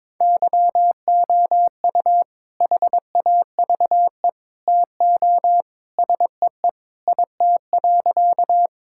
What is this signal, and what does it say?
Morse code, heard as a keyed tone, about 22 wpm, 700 Hz. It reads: YOU HAVE TO SEE IT.
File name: practice / 22wpm / Sentences with 2-Word Phrases - 22wpm